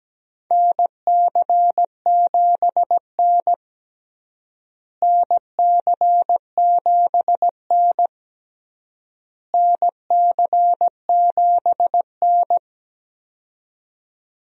Morse code, heard as a keyed tone, about 17 wpm, 700 Hz. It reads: NC7N NC7N NC7N
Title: USA Call Signs - 4 Characters - Repeated 3x - 17wpm